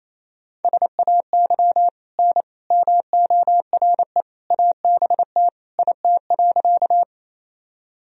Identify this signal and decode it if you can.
Morse code, heard as a keyed tone, about 28 wpm, 700 Hz. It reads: SAY N MORE ABT IT.
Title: Sentences from Top 200 Words and Abbreviations 28wpm